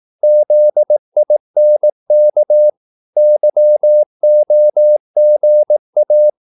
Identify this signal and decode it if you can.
Morse code, heard as a keyed tone, about 18 wpm, 600 Hz. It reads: ZINK YOGA
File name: Korte ord DK 18wpm